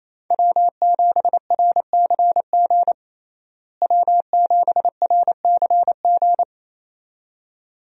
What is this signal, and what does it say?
Morse code, heard as a keyed tone, about 28 wpm, 700 Hz. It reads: W7RCG W7RCG
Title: USA Call Signs - 5 Characters - Repeated 2x - 28wpm